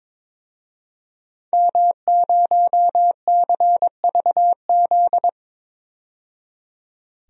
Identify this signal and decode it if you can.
Morse code, heard as a keyed tone, about 22 wpm, 700 Hz. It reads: M0CVZ
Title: DXCC-100 Call Signs - 5 Characters - 22wpm